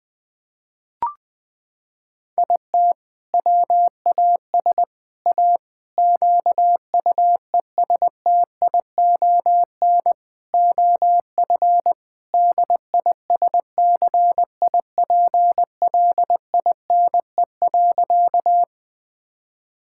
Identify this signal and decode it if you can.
Morse code, heard as a keyed tone, about 20 wpm, 700 Hz. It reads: IT WAS A QUESTION OF DISCIPLINE.
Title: Sentences from Top 1000 Words - Review Part 6 - 20wpm